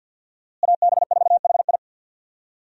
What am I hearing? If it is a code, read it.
Morse code, heard as a keyed tone, about 51 wpm, 700 Hz. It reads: AB4HI
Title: USA Call Signs - 5 Characters - Repeated 3x - 50wpm